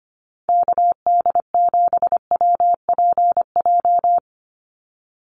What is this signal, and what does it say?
Morse code, heard as a keyed tone, about 25 wpm, 700 Hz. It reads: KD7WPJ
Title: NA SOTA Activator Callsigns 25wpm